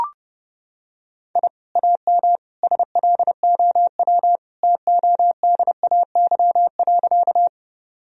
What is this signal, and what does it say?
Morse code, heard as a keyed tone, about 30 wpm, 700 Hz. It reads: I AM SLOW TODAY.
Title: Sentences from Top 1000 Words - Review Part 2 - 30wpm